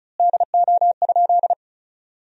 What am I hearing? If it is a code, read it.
Morse code, heard as a keyed tone, about 34 wpm, 700 Hz. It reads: DO?
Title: 4x Word-Spacing - Sentences from Top 200 Words - 35wpm